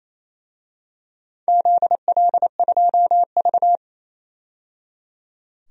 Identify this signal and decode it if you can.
Morse code, heard as a keyed tone, about 28 wpm, 700 Hz. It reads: ZL2V
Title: DXCC-100 Call Signs - 4 Characters - 28wpm